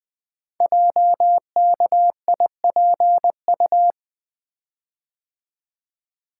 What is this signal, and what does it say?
Morse code, heard as a keyed tone, about 20 wpm, 700 Hz. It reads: JKIPU
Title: Code Groups - 5 Letters - 20wpm